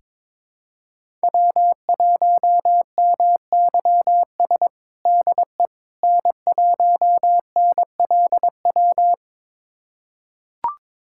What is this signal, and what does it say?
Morse code, heard as a keyed tone, about 21 wpm, 700 Hz. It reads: W1MYS DE N1NLW
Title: USA Callsign DE Callsign - 5 Characters - 22wpm